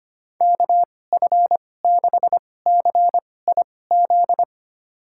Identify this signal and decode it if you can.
Morse code, heard as a keyed tone, about 25 wpm, spaced as for 18 wpm, 700 Hz. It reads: KF6CIZ
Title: USA Call Signs - 6 Characters - 2x Character Spacing - 25wpm